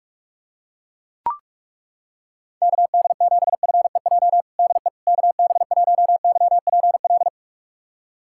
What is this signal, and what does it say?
Morse code, heard as a keyed tone, about 41 wpm, 700 Hz. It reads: KD7FEJ DE KB1YPL
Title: USA Callsign DE Callsign - 6 Characters - 45wpm